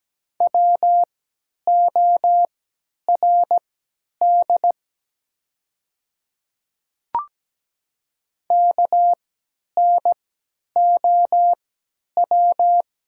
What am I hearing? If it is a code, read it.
Morse code, heard as a keyed tone, about 17 wpm, spaced as for 10 wpm, 700 Hz. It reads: WORD KNOW
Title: 3x Character Spacing - Top 200 Words - 17wpm